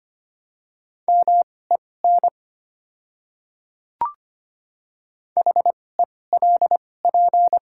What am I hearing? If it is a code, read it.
Morse code, heard as a keyed tone, about 25 wpm, spaced as for 18 wpm, 700 Hz. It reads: MEN HELP
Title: 2x Character Spacing - Top 300 Words - 25wpm